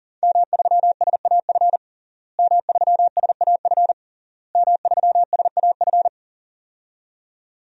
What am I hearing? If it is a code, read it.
Morse code, heard as a keyed tone, about 41 wpm, 700 Hz. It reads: M3SAF M3SAF M3SAF
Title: DXCC-100 Call Signs - 5 Characters - Repeated 3x - 40wpm